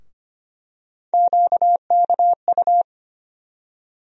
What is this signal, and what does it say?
Morse code, heard as a keyed tone, about 25 wpm, 700 Hz. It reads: QKU
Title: USA Call Sign Postfixes - 3 Letters 25wpm